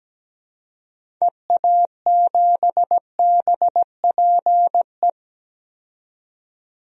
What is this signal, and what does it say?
Morse code, heard as a keyed tone, about 17 wpm, 700 Hz. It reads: EA7BPE
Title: DXCC-100 Call Signs - 6 Characters - 17wpm